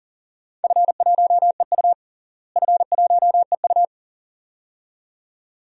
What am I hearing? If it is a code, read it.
Morse code, heard as a keyed tone, about 40 wpm, 700 Hz. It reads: F1EU F1EU
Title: DXCC-100 Call Signs - 4 Characters - Repeated 2x - 40wpm